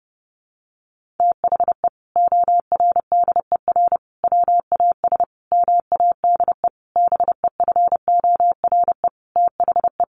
Morse code, heard as a keyed tone, about 30 wpm, 700 Hz. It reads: THE ORDER WAS MADE BEFORE THE
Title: Sentences from Top 1500 Words 30wpm